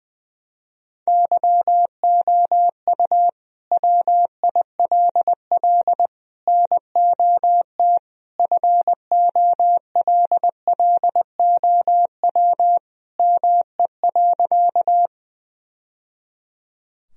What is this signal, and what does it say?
Morse code, heard as a keyed tone, about 20 wpm, 700 Hz. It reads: YOU WILL NOT FOLLOW ME.
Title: Sentences from Top 1000 Words - Review Part 1 - 20wpm